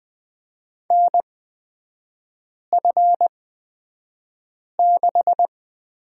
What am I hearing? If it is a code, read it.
Morse code, heard as a keyed tone, about 20 wpm, spaced as for 5 wpm, 700 Hz. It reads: NF6
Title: 224 Lesson 36e - BT - Three Characters 20wpm